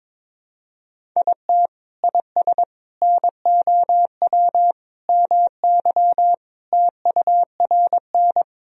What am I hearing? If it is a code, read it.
Morse code, heard as a keyed tone, about 22 wpm, 700 Hz. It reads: IT IS NOW MY TURN
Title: Sentences from Top 200 Words and Abbreviations 22wpm